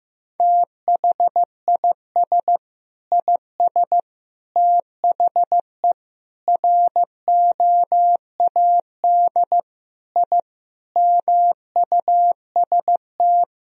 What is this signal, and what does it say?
Morse code, heard as a keyed tone, about 16 wpm, 700 Hz. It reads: THIS IS THE ROAD I MUST
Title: Sentences from Top 1000 Words - Review Part 2 - 15wpm